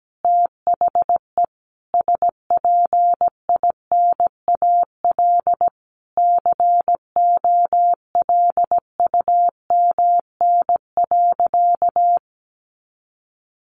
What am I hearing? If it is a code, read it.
Morse code, heard as a keyed tone, about 17 wpm, 700 Hz. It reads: THE SPINAL COLUMN.
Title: A Sound Of Thunder 17wpm